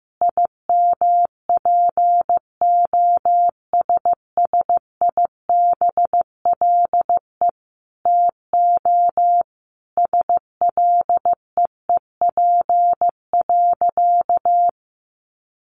Morse code, heard as a keyed tone, about 15 wpm, 700 Hz. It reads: IMPOSSIBLE TO SLEEP.
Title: Sentences from Top 1100 Words 15wpm